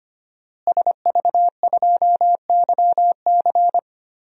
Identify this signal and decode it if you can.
Morse code, heard as a keyed tone, about 25 wpm, 700 Hz. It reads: SV2YC
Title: DXCC-100 Call Signs - 5 Characters - Repeated 3x - 25wpm